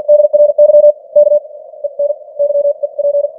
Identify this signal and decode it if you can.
Morse code, heard as a keyed tone, about 50 wpm, spaced as for 43 wpm, 590 Hz. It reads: DRY K EN YEP
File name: archived / CWBros